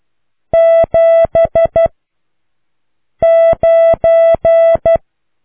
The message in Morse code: 7 9